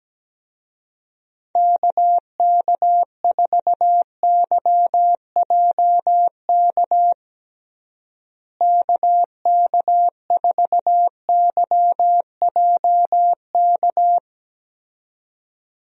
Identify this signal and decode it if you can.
Morse code, heard as a keyed tone, about 17 wpm, 700 Hz. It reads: KK4YJK KK4YJK
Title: USA Call Signs - 6 Characters - Repeated 2x - 17wpm